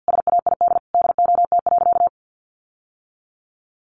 Morse code, heard as a keyed tone, about 50 wpm, 700 Hz. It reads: SAID DOT.
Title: Sentences from Top 1500 Words 50wpm